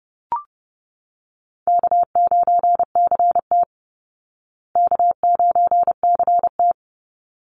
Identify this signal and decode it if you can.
Morse code, heard as a keyed tone, about 30 wpm, 700 Hz. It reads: K9CT K9CT
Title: Call Signs Repeated 3x - 30wpm